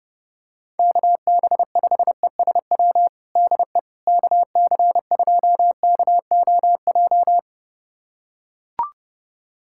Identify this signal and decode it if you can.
Morse code, heard as a keyed tone, about 30 wpm, 700 Hz. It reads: KB5ESW DE KC2KOJ
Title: USA Callsign DE Callsign 30wpm